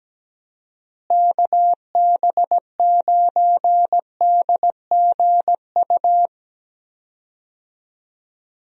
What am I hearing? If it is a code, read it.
Morse code, heard as a keyed tone, about 17 wpm, 700 Hz. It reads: KB9DGU
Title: USA Call Signs - 6 Characters - Repeated 3x - 17wpm